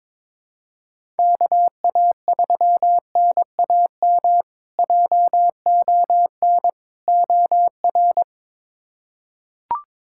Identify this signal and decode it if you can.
Morse code, heard as a keyed tone, about 22 wpm, 700 Hz. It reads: KA3NAM JON OR
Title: K1USN SST QSO Practice - 22wpm